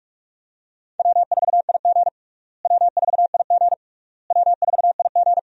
Morse code, heard as a keyed tone, about 46 wpm, 700 Hz. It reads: W4IG W4IG W4IG
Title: USA Call Signs - 4 Characters - Repeated 3x - 45wpm